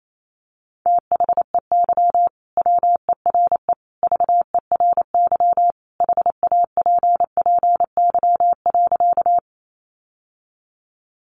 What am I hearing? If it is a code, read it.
Morse code, heard as a keyed tone, about 28 wpm, 700 Hz. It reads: THEY WERE VERY HAPPY.
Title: Sentences from Top 1000 Words 28wpm